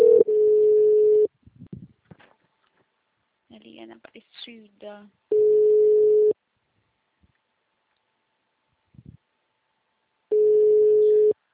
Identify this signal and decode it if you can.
Ringback tone (Europe, 425 Hz), with speech.
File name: out- -2002- - - .65